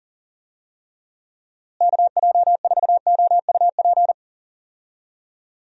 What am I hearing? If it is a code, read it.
Morse code, heard as a keyed tone, about 40 wpm, 700 Hz. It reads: KJ4OUP
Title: USA Call Signs - 6 Characters - Repeated 3x - 40wpm